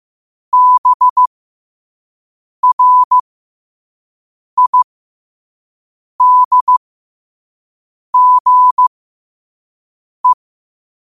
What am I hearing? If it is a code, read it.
Morse code, heard as a keyed tone, about 15 wpm, spaced as for 5 wpm, 1000 Hz. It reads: BRIDGE
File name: qotdfiles / 5wpm